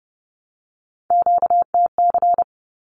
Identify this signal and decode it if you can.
Morse code, heard as a keyed tone, about 30 wpm, 700 Hz. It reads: QTC
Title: Q-Codes 30wpm